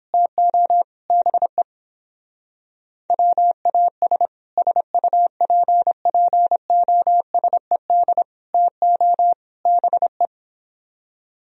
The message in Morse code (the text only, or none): TO BE WAS SUPPOSED TO BE